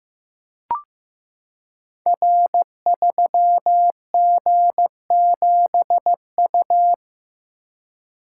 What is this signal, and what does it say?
Morse code, heard as a keyed tone, about 15 wpm, 700 Hz. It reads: R3G7U
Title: Code Groups - 5 Letter:Numbers - 15wpm